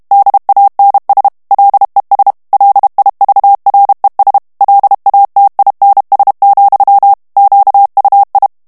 Morse code, heard as a keyed tone, about 32 wpm, 800 Hz. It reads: DANS LES LIVRES LATINS, QUI
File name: FRANCAIS TEXTE CLAIR 2 VITESSE 30 MOTS MINUTES